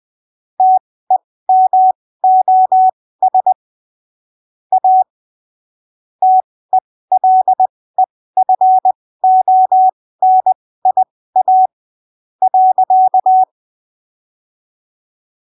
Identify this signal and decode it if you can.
Morse code, heard as a keyed tone, about 20 wpm, spaced as for 15 wpm, 750 Hz. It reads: TEMOS A TELEFONIA .
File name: arquivos / texto / texto 04 20ppm